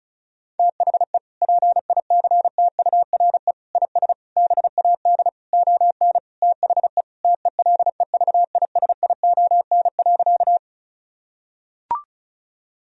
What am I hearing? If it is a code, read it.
Morse code, heard as a keyed tone, about 36 wpm, 700 Hz. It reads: THE PICTURE IS BAD ON THE TELEVISION.
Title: Sentences from Top 1000 Words - Review Part 4 - 35wpm